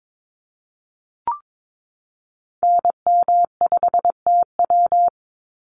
Morse code, heard as a keyed tone, about 22 wpm, 700 Hz. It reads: NM5TW
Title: NA SOTA Activator Callsigns 22wpm